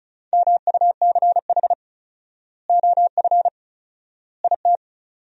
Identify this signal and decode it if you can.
Morse code, heard as a keyed tone, about 36 wpm, 700 Hz. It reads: MUCH OF IT
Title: 4x Word-Spacing - Sentences from Top 400 Words - 35wpm